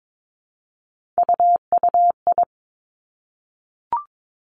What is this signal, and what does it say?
Morse code, heard as a keyed tone, about 22 wpm, 700 Hz. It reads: UUI
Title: Three Challenging Letter-Number Combinations 22wpm